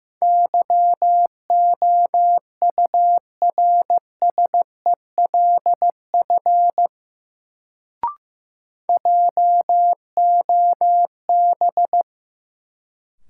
Morse code, heard as a keyed tone, about 16 wpm, 700 Hz. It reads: YOURSELF JOB
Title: Sentences from Top 600 Words - New Word Drill - 15wpm